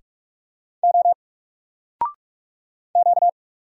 Morse code, heard as a keyed tone, about 44 wpm, 700 Hz. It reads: O Q
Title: Single Character (Letter:Number:Punctuation:Prosign) Alt 45wpm